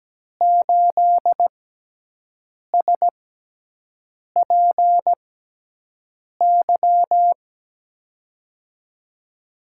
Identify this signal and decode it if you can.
Morse code, heard as a keyed tone, about 17 wpm, spaced as for 6 wpm, 700 Hz. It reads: 8SPY
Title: USA Call Signs - 5 Characters - 6x Character Spacing - 17wpm